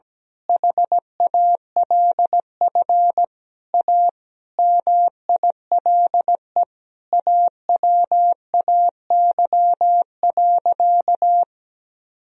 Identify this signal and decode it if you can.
Morse code, heard as a keyed tone, about 17 wpm, 700 Hz. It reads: HALF A MILE AWAY.
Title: Sentences from Top 1000 Words - Review Part 2 - 17wpm